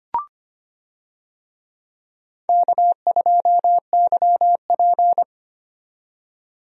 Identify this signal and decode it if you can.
Morse code, heard as a keyed tone, about 25 wpm, 700 Hz. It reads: K2YP
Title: USA Call Signs - 4 Characters - 25wpm